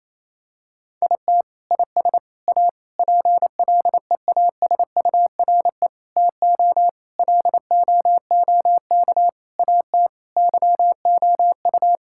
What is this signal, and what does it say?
Morse code, heard as a keyed tone, about 28 wpm, 700 Hz. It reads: IT IS A PLEASURE TO LOOK AT YOU
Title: Sentences from Top 1600 Words 28wpm